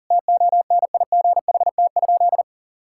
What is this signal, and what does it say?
Morse code, heard as a keyed tone, about 40 wpm, 700 Hz. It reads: TONIGHT?
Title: Sentences from Top 1000 Words - Review Part 6 - 40wpm